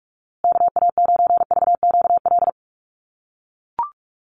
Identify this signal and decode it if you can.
Morse code, heard as a keyed tone, about 44 wpm, 700 Hz. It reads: KA9VQL
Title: USA Call Signs Repeated 3x 45wpm